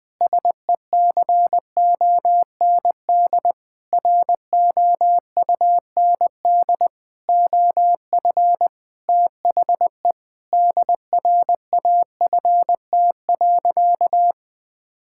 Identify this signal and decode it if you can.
Morse code, heard as a keyed tone, about 21 wpm, 700 Hz. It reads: SECOND ROUND OF THE DRAFT.
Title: Sentences with 6-Word Phrases - 20wpm